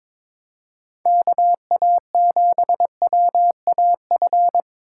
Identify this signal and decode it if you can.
Morse code, heard as a keyed tone, about 22 wpm, 700 Hz. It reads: KA7WAF
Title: Speed Racing - USA 2x3 Call Signs 22wpm